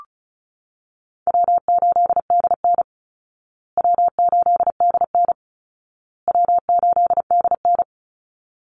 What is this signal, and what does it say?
Morse code, heard as a keyed tone, about 34 wpm, 700 Hz. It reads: W8DN W8DN W8DN
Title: Call Signs Repeated 2x - 35wpm